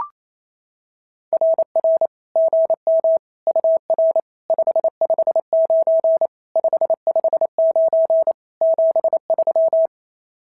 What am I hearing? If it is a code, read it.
Morse code, heard as a keyed tone, about 28 wpm, 650 Hz. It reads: RR GM UR 559 559 73
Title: SOTA QSO Practice 28wpm